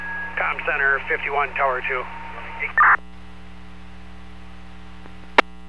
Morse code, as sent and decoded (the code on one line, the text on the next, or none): none